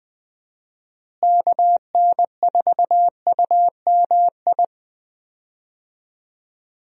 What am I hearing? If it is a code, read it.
Morse code, heard as a keyed tone, about 20 wpm, 700 Hz. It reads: KN4UMI
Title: USA Call Signs - 6 Characters - Repeated 3x - 20wpm